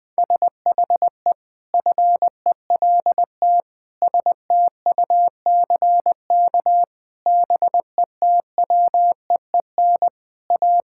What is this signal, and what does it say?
Morse code, heard as a keyed tone, about 20 wpm, 700 Hz. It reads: SHE FELT STUCK BETWEEN A
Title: Sentences with 8-Word Phrases - 20wpm